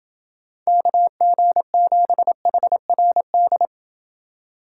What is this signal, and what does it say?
Morse code, heard as a keyed tone, about 29 wpm, 700 Hz. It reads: KG7HRD